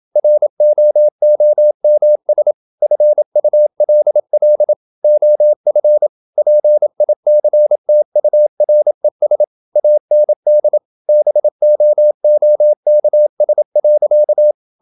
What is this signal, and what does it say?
Morse code, heard as a keyed tone, about 27 wpm, 600 Hz. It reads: ROOMS FULL OF PICTURES AND BOOKS.
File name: Never 0027wpm